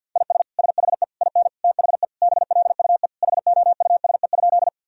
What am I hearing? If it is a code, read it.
Morse code, heard as a keyed tone, about 46 wpm, 700 Hz. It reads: IS SHE IN THE BLUE HOUSE?
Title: Sentences from Top 1000 Words - Review Part 3 - 50wpm